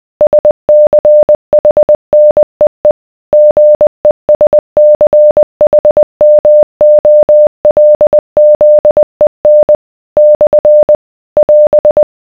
Morse code, heard as a keyed tone, about 20 wpm, 600 Hz. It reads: SCHNEE GESCHMOLZEN / <AS>